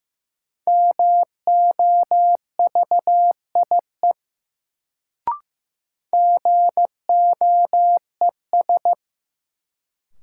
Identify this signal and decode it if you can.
Morse code, heard as a keyed tone, about 15 wpm, 700 Hz. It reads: MOVIE GOES
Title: Sentences from Top 1000 Words - Review Part 5 - New Word Drill - 15wpm